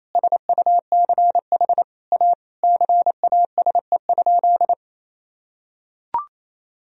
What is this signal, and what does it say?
Morse code, heard as a keyed tone, about 27 wpm, 700 Hz. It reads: SUCH A CASE?
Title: Sentences from Top 1000 Words - Review Part 2 - 28wpm